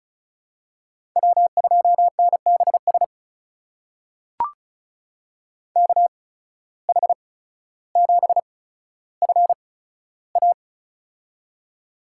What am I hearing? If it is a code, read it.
Morse code, heard as a keyed tone, about 36 wpm, spaced as for 15 wpm, 700 Hz. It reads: W2NBS KH7FA
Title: USA Call Signs - 5 Characters - 8x Character Spacing - 35wpm